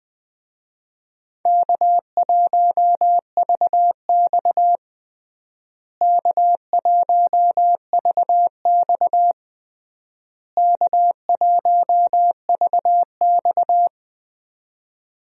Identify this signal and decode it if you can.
Morse code, heard as a keyed tone, about 20 wpm, 700 Hz. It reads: K1VX K1VX K1VX
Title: USA Call Signs - 4 Characters - Repeated 3x - 20wpm